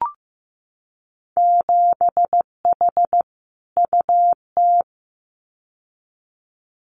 Morse code, heard as a keyed tone, about 15 wpm, 700 Hz. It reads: 7H UT
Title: Winter Field Day Exchanges Encore 15wpm